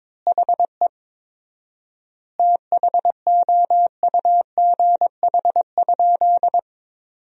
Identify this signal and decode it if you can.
Morse code, heard as a keyed tone, about 22 wpm, 700 Hz. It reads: HE THOUGH?
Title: 4x Word-Spacing - Sentences from Top 400 Words - 22wpm